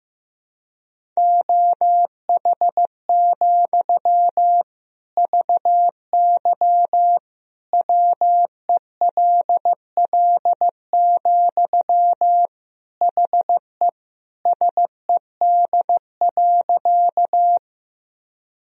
Morse code, heard as a keyed tone, about 15 wpm, 700 Hz. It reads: OH, VY WELL, HE SED.
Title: Sentences from Top 500 Words and Abbreviations 15wpm